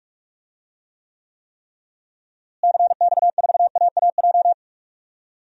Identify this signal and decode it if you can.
Morse code, heard as a keyed tone, about 44 wpm, 700 Hz. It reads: CX4AAJ
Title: DXCC-100 Call Signs - 6 Characters - Repeated 3x - 45wpm